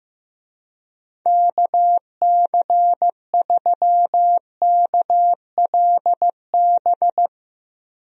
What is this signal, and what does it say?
Morse code, heard as a keyed tone, about 15 wpm, 700 Hz. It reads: KC3KLB
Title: USA Call Signs - 6 Characters - Repeated 3x - 15wpm